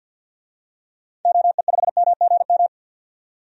Morse code, heard as a keyed tone, about 46 wpm, 700 Hz. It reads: OE5MGM
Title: DXCC-100 Call Signs - 6 Characters - 50wpm